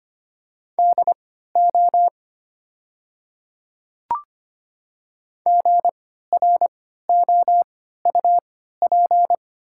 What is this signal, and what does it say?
Morse code, heard as a keyed tone, about 25 wpm, spaced as for 14 wpm, 700 Hz. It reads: DO GROUP
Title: 3x Character Spacing - Top 500 Words - 25wpm